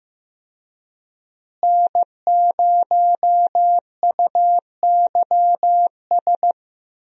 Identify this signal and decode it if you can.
Morse code, heard as a keyed tone, about 15 wpm, 700 Hz. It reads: N0UYS
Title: USA Call Signs - 5 Characters - Repeated 3x - 15wpm